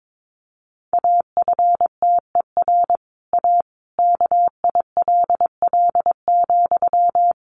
Morse code, heard as a keyed tone, about 22 wpm, 700 Hz. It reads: AFTER A KILL,
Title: Sentences from Top 1500 Words 22wpm